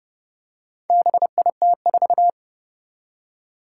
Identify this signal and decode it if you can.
Morse code, heard as a keyed tone, about 30 wpm, 700 Hz. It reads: BIT4
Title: Four Challenging Letter-Number Combinations 30wpm